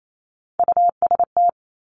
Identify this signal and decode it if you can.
Morse code, heard as a keyed tone, about 28 wpm, 690 Hz. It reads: UST
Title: Top Three Letter Combinations in Words 28wpm